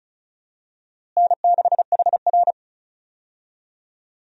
Morse code, heard as a keyed tone, about 34 wpm, 700 Hz. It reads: N6HR
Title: USA Call Signs - 4 Characters - Repeated 2x - 35wpm